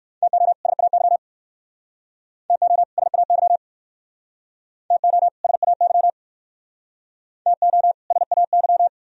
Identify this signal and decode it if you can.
Morse code, heard as a keyed tone, about 57 wpm, 700 Hz. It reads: TO SAY TO SAY TO SAY TO SAY